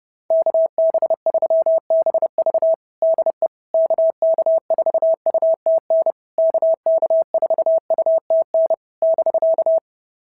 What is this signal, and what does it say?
Morse code, heard as a keyed tone, about 30 wpm, 660 Hz. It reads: KB3BV DE KK4UTN KK4UTN <BK>
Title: SOTA QSO Practice 30wpm